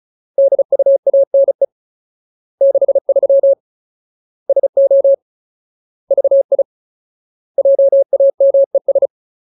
Morse code, heard as a keyed tone, about 34 wpm, 550 Hz. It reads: DUANE 63 SO VI JAMES
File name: Ultimate Word List / 35WPM Ultimate